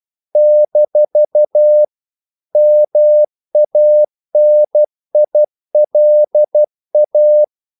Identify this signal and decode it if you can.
Morse code, heard as a keyed tone, about 12 wpm, 600 Hz. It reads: - MANILA
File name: CW / Practice / NationalCapitals12